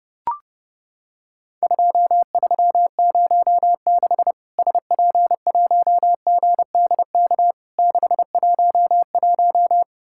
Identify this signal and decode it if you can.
Morse code, heard as a keyed tone, about 30 wpm, 700 Hz. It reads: 2306 SP1GDK 611
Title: Worked All Europe DX Contest - QTC Exchange - 30wpm